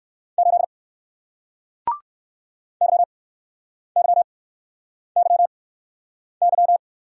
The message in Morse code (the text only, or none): 8 YYYY